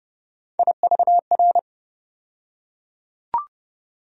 Morse code, heard as a keyed tone, about 29 wpm, 700 Hz. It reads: IVR
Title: USA Call Sign Postfixes - 3 Letters 30wpm